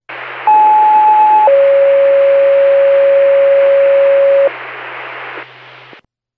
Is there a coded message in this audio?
Yes: two-tone page from 0.5 to 4.5 s, heard as a one-second tone, then a three-second tone, about 860 then 550 Hz, with noise.